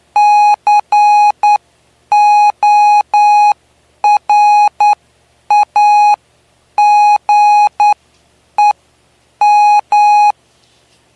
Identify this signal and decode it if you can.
Morse code, heard as a keyed tone, about 9 wpm, spaced as for 8 wpm, 830 Hz. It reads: CORAGEM
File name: cw / py2lcd / 09-PY2LCD-Li��o 9